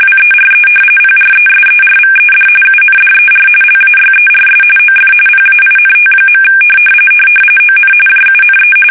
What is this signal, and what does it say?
RTTY, about 45.45 baud. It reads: WERE PRESERVED UNDER LAYERS OF PAINT. ON JUNE
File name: fsk850 / battleshipnj